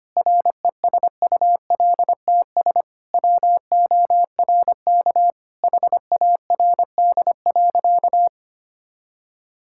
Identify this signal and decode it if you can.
Morse code, heard as a keyed tone, about 25 wpm, 700 Hz. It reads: RESULTS WORK HARD.
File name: practice / 25wpm / Sentences with 3-Word Phrases - 25wpm